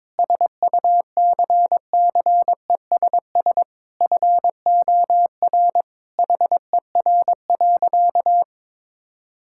Morse code, heard as a keyed tone, about 22 wpm, 700 Hz. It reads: SUCCESS FOR HER.
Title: Sentences from Top 1000 Words - Review Part 6 - 22wpm